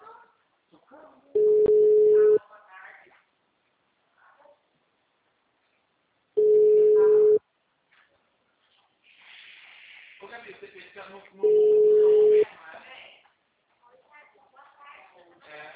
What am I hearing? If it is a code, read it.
Ringback tone (Europe, 425 Hz), with speech.